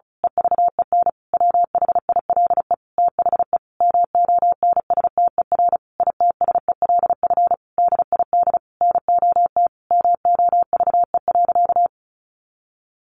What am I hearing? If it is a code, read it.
Morse code, heard as a keyed tone, about 34 wpm, 700 Hz. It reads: EVEN WHILE THE MONSTER ITSELF DID NOT MOVE.
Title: A Sound Of Thunder 35wpm